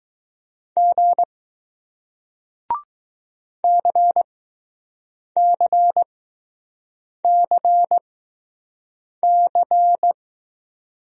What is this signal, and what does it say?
Morse code, heard as a keyed tone, about 20 wpm, spaced as for 6 wpm, 700 Hz. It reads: G CCCC